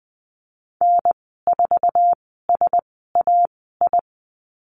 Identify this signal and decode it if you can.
Morse code, heard as a keyed tone, about 20 wpm, spaced as for 14 wpm, 700 Hz. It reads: N4SAI
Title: 076 Lesson 13g - Letter L - Call Signs 20wpm